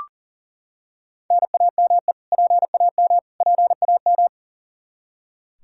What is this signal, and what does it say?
Morse code, heard as a keyed tone, about 40 wpm, 700 Hz. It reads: NAME PAM PAM
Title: Practice QSO - Exchange 1 - Beginning - 40wpm